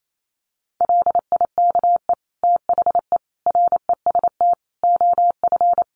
Morse code, heard as a keyed tone, about 28 wpm, 700 Hz. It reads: LIKE THE REST OF
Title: Sentences from Top 1400 Words 28wpm